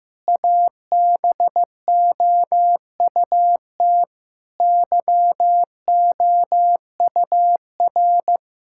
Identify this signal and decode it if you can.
Morse code, heard as a keyed tone, about 15 wpm, 700 Hz. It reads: ABOUT YOUR
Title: Sentences from Top 1000 Words - Review Part 6 - 15wpm